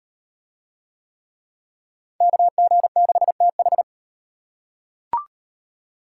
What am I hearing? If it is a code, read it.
Morse code, heard as a keyed tone, about 34 wpm, 700 Hz. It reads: KG6TH